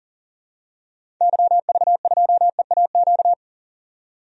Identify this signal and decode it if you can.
Morse code, heard as a keyed tone, about 40 wpm, 700 Hz. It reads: YV2EAQ
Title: DXCC-100 Call Signs - 6 Characters - 40wpm